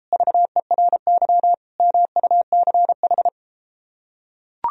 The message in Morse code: VERY MUCH

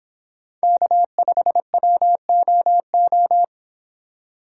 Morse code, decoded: K5WOO

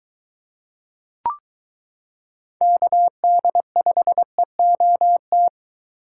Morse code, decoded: KD5EOT

